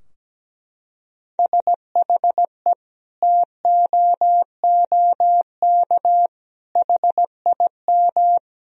Morse code, heard as a keyed tone, about 17 wpm, 700 Hz. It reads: SHE TOOK HIM
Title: Sentences from Top 1000 Words - Review Part 4 - 17wpm